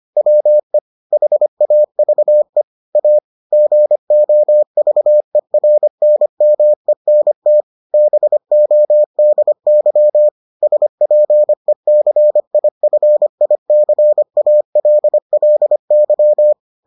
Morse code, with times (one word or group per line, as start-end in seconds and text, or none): 0.2-0.8 WE
1.1-2.6 HAVE
3.0-3.2 A
3.5-7.6 GOVERNMENT
7.9-10.3 BODY
10.6-16.5 SPECIFICALLY